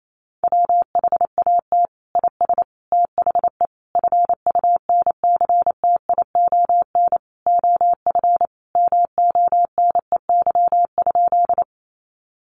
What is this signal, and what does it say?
Morse code, heard as a keyed tone, about 28 wpm, 700 Hz. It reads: WHAT IS THE FUNCTION OF MONEY?
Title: Sentences from Top 1100 Words 28wpm